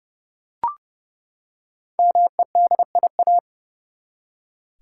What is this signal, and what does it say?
Morse code, heard as a keyed tone, about 30 wpm, 700 Hz. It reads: MEDIA
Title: Five Letter Words - 30wpm